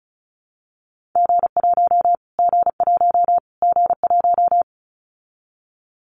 Morse code, heard as a keyed tone, about 34 wpm, 700 Hz. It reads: G1 G1 G1
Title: Callsign Prefixes for Top 25 DXCC - Repeated 3x - 35wpm